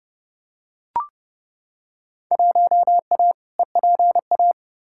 Morse code, heard as a keyed tone, about 30 wpm, 700 Hz. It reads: 1A EPA
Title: Field Day Exchanges - 30wpm